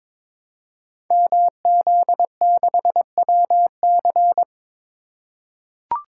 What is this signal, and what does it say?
Morse code, heard as a keyed tone, about 22 wpm, 700 Hz. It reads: MZ6WC
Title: Code Groups - 5 Letter:Numbers - 22wpm